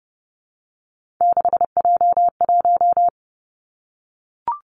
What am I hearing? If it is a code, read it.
Morse code, heard as a keyed tone, about 30 wpm, 700 Hz. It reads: 6J1
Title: Three Challenging Letter-Number Combinations 30wpm